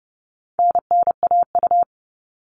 Morse code, heard as a keyed tone, about 29 wpm, 700 Hz. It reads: NNAU
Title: All Cut Numbers - 4-Digit 30wpm